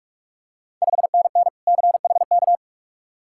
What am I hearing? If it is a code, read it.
Morse code, heard as a keyed tone, about 47 wpm, 690 Hz. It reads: 5NN CHK
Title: Ontario QSO Party Exchange 45wpm